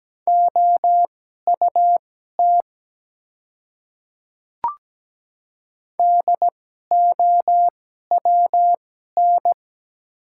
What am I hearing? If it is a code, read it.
Morse code, heard as a keyed tone, about 17 wpm, spaced as for 12 wpm, 700 Hz. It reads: OUT DOWN
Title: 2x Character Spacing - Top 100 Words - 17wpm